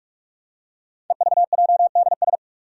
Morse code, heard as a keyed tone, about 44 wpm, 700 Hz. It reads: EVJDS
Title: Code Groups - 5 Letters - 45wpm